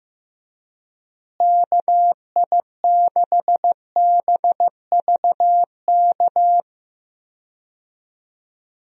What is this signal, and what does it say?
Morse code, heard as a keyed tone, about 15 wpm, 700 Hz. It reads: KI6BVK
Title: USA Call Signs - 6 Characters - Repeated 3x - 15wpm